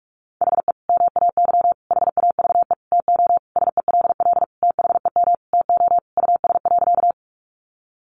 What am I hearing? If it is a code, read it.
Morse code, heard as a keyed tone, about 46 wpm, 700 Hz. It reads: HE MAY HAVE TO SELL THEM TO US.
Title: Sentences from Top 1100 Words 45wpm